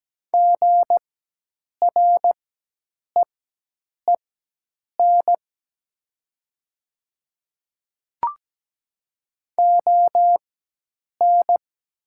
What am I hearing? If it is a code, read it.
Morse code, heard as a keyed tone, about 17 wpm, spaced as for 8 wpm, 700 Hz. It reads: GREEN ON
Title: 4x Character Spacing - Top 500 Words - 17wpm